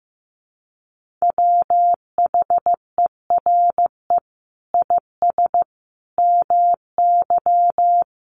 Morse code, heard as a keyed tone, about 15 wpm, 700 Hz. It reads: WHERE IS MY
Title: Sentences from Top 1400 Words 15wpm